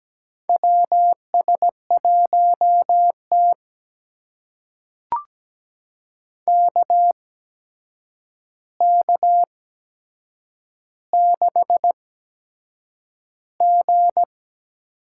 Morse code, heard as a keyed tone, about 17 wpm, spaced as for 7 wpm, 700 Hz. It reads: WS1T KK6G
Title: USA Call Signs - 8x Character Spacing - 17wpm